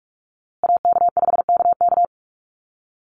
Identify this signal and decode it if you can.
Morse code, heard as a keyed tone, about 44 wpm, 700 Hz. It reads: AK5KK
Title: USA Call Signs Repeated 2x 45wpm